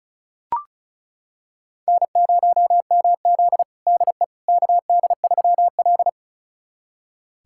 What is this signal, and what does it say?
Morse code, heard as a keyed tone, about 36 wpm, 700 Hz. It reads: N0MZ DE KD3L
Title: USA Callsign DE Callsign - 4 Characters - 35wpm